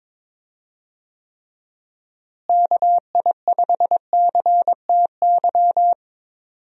Morse code, heard as a keyed tone, about 22 wpm, 700 Hz. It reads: KI5CTY
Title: USA Call Signs - 6 Characters - Repeated 3x - 22wpm